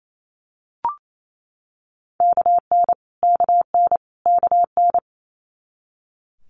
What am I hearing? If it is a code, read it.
Morse code, heard as a keyed tone, about 28 wpm, 700 Hz. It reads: KN KN KN
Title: Multi-letter Phonemes Repeated 3x 28wpm